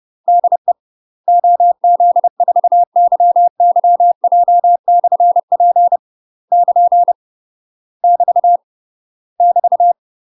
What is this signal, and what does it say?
Morse code, heard as a keyed tone, about 30 wpm, 700 Hz. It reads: DE OZ4YYJ/P <KN> <BT> <BT>
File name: UK MASTER QSOs TRAINING / 30WPM / UK MASTER QSOs 0018